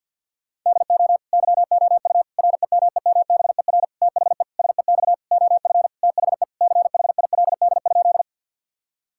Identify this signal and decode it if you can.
Morse code, heard as a keyed tone, about 50 wpm, 700 Hz. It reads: DO YOU REMEMBER THE SEX OF THE CHILD?
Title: Sentences from Top 1600 Words 50wpm